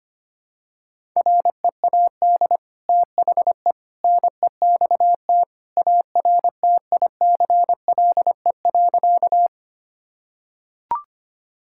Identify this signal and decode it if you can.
Morse code, heard as a keyed tone, about 25 wpm, 700 Hz. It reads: READ THE NEXT ARTICLE.
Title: Sentences from Top 1000 Words - Review Part 5 - 25wpm